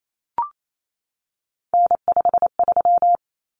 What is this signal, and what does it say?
Morse code, heard as a keyed tone, about 28 wpm, 700 Hz. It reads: N53
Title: Common Cut Numbers - 3-Digit 28wpm